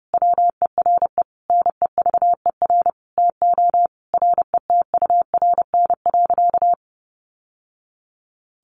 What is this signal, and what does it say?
Morse code, heard as a keyed tone, about 30 wpm, 700 Hz. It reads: WERE NEVER TO RETURN.
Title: Sentences from Top 800 Words 30wpm